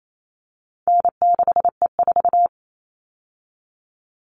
Morse code, heard as a keyed tone, about 28 wpm, 700 Hz. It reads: N6E4
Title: All Cut Numbers - 4-Digit 28wpm